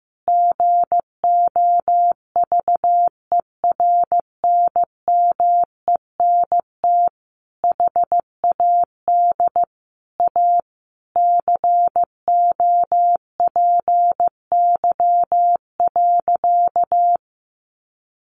Morse code, heard as a keyed tone, about 15 wpm, 700 Hz. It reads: GOVERNMENT HAD A COPY.
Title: Sentences from Top 1400 Words 15wpm